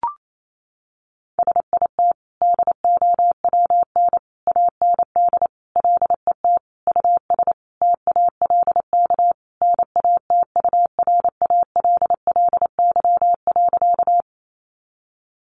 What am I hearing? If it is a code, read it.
Morse code, heard as a keyed tone, about 28 wpm, 700 Hz. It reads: SIT DOWN AND LET US TALK NATURALLY.
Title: Sentences from Top 1300 Words 28wpm